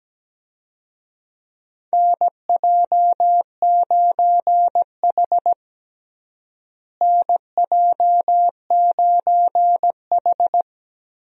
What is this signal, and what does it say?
Morse code, heard as a keyed tone, about 17 wpm, 700 Hz. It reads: NJ9H NJ9H
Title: USA Call Signs - 4 Characters - Repeated 3x - 17wpm